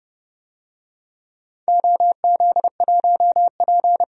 Morse code, heard as a keyed tone, about 30 wpm, 700 Hz. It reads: OZ1P
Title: DXCC-100 Call Signs - 4 Characters - 30wpm